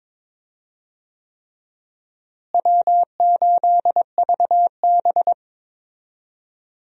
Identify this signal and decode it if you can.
Morse code, heard as a keyed tone, about 22 wpm, 700 Hz. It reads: W8VB
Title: USA Call Signs - 4 Characters - Repeated 3x - 22wpm